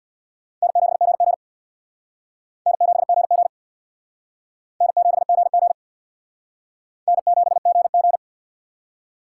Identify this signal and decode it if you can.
Morse code, heard as a keyed tone, about 64 wpm, 700 Hz. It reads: N7GG N7GG N7GG N7GG